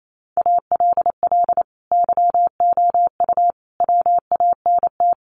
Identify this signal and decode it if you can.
Morse code, heard as a keyed tone, about 28 wpm, 700 Hz. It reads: ALL YOU WANT
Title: Sentences from Top 1200 Words 28wpm